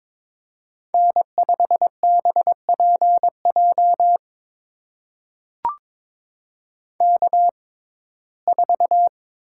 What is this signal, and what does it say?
Morse code, heard as a keyed tone, about 22 wpm, spaced as for 16 wpm, 700 Hz. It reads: N5BPJ K4
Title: USA Call Signs - 5 Characters - 6x Character Spacing - 22wpm